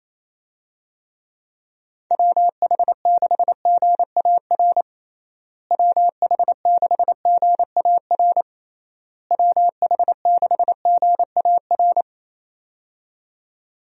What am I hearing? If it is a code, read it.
Morse code, heard as a keyed tone, about 28 wpm, 700 Hz. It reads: WH6GAR WH6GAR WH6GAR
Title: USA Call Signs - 6 Characters - Repeated 3x - 28wpm